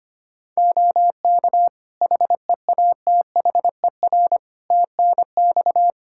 Morse code, heard as a keyed tone, about 25 wpm, 700 Hz. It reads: OK HEATHER TNX
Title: K1USN SST QSO Practice 25wpm